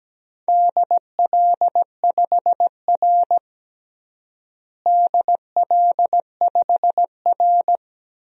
Morse code, heard as a keyed tone, about 17 wpm, 700 Hz. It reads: DL5R DL5R
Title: DXCC-100 Call Signs - 4 Characters - Repeated 3x - 17wpm